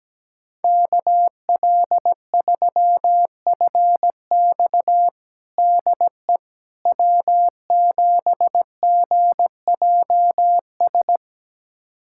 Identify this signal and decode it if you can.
Morse code, heard as a keyed tone, about 17 wpm, 700 Hz. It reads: KL3FX DE W7GJS
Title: USA Callsign DE Callsign - 5 Characters - 17wpm